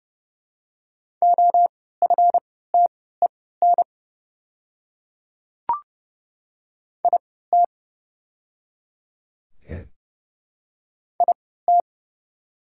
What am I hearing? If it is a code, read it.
Morse code, heard as a keyed tone, about 30 wpm, spaced as for 17 wpm, 690 Hz. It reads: OFTEN IT IT
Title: 3x Character Spacing - Top 300 Words - 30wpm